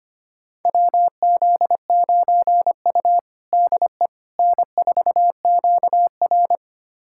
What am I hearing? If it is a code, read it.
Morse code, heard as a keyed tone, about 25 wpm, 700 Hz. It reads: WZ9U DE N4QR
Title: USA Callsign DE Callsign - 4 Characters - 25wpm